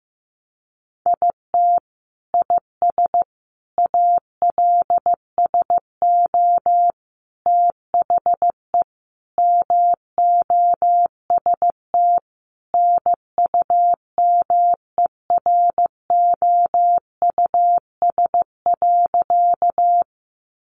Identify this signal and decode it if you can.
Morse code, heard as a keyed tone, about 14 wpm, 700 Hz. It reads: IT IS ALSO THE MOST NUMEROUS.
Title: Sentences from Top 1400 Words 15wpm